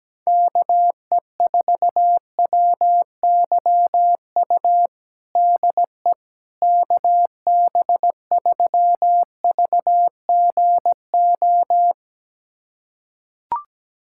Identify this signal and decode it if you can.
Morse code, heard as a keyed tone, about 17 wpm, 700 Hz. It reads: KE4WYU DE KB3VGO
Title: USA Callsign DE Callsign - 6 Characters - 17wpm